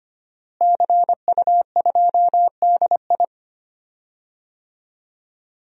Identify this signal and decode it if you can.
Morse code, heard as a keyed tone, about 25 wpm, 700 Hz. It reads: CU2DI
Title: DXCC-100 Call Signs - 5 Characters - Repeated 2x - 25wpm